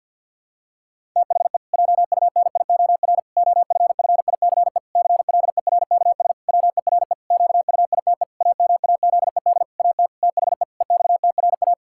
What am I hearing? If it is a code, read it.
Morse code, heard as a keyed tone, about 50 wpm, 700 Hz. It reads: THE JUNIOR OFFICE CLERKS WERE QUITE AMAZED AT THE EXTRA
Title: Pangram Challenge v2 - 50wpm